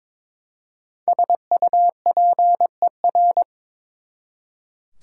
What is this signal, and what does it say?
Morse code, heard as a keyed tone, about 22 wpm, 700 Hz. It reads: SUPER
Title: Prefixes and Suffixes 22wpm